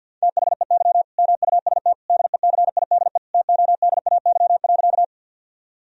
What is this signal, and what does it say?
Morse code, heard as a keyed tone, about 50 wpm, 700 Hz. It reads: THEY MUST DECIDE TODAY.
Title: Sentences from Top 1000 Words - Review Part 3 - 50wpm